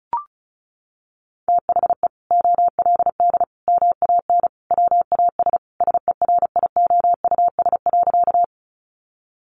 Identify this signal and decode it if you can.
Morse code, heard as a keyed tone, about 34 wpm, 700 Hz. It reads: THE OLD MAN WAS SERIOUS.
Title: Sentences from Top 1100 Words 35wpm